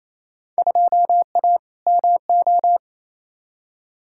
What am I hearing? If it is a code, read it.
Morse code, heard as a keyed tone, about 28 wpm, 700 Hz. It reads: 2A MO
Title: Field Day Exchanges Encore - 28wpm